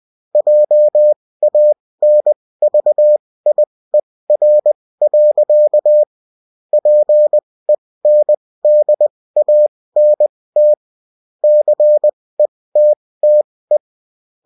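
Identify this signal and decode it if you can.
Morse code, heard as a keyed tone, about 20 wpm, spaced as for 16 wpm, 600 Hz. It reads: JANVIER. PENDANT CETTE
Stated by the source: Bulletin in text and CW form.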